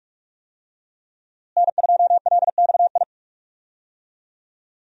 Morse code, heard as a keyed tone, about 46 wpm, 700 Hz. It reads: N2LXI
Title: USA Call Signs - 5 Characters - Repeated 3x - 45wpm